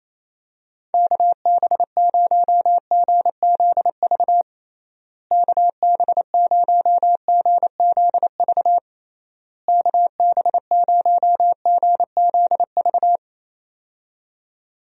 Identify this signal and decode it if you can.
Morse code, heard as a keyed tone, about 28 wpm, 700 Hz. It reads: KB0GZV KB0GZV KB0GZV
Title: USA Call Signs - 6 Characters - Repeated 3x - 28wpm